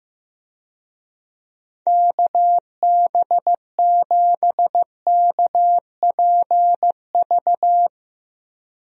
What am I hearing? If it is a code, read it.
Morse code, heard as a keyed tone, about 15 wpm, 700 Hz. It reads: KB7KPV
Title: USA Call Signs - 6 Characters - Repeated 3x - 15wpm